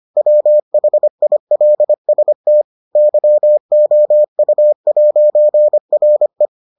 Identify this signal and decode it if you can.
Morse code, heard as a keyed tone, about 25 wpm, 600 Hz. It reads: WHILST YOU'RE